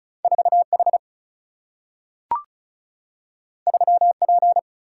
Morse code, heard as a keyed tone, about 36 wpm, 700 Hz. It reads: 4 H 3 P
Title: Single-Letter Single-Number 35wpm